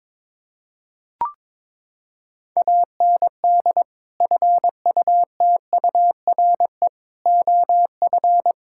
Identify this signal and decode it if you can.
Morse code, heard as a keyed tone, about 22 wpm, 700 Hz. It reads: AND FUTURE OF
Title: Sentences from Top 1600 Words 22wpm